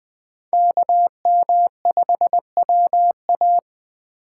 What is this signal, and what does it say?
Morse code, heard as a keyed tone, about 20 wpm, 700 Hz. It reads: KM5WA
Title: USA Call Signs - 5 Characters - 2x Character Spacing - 20wpm